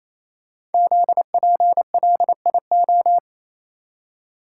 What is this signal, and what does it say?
Morse code, heard as a keyed tone, about 29 wpm, 700 Hz. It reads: ZPLIO
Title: Code Groups - 5 Letters - 28wpm